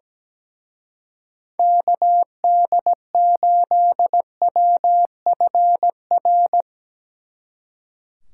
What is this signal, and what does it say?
Morse code, heard as a keyed tone, about 20 wpm, 700 Hz. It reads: KD8WFR
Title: Speed Racing - USA 2x3 Call Signs 17wpm